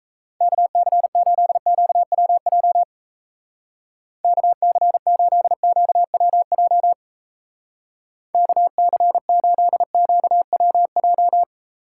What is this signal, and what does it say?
Morse code, heard as a keyed tone, about 34 wpm, 700 Hz. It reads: KC8QWJ KC8QWJ KC8QWJ
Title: Speed Racing - USA 2x3 Call Signs 28wpm